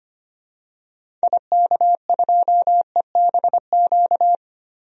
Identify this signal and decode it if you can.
Morse code, heard as a keyed tone, about 25 wpm, 700 Hz. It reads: IK2EBQ
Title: DXCC-100 Call Signs - 6 Characters - Repeated 3x - 25wpm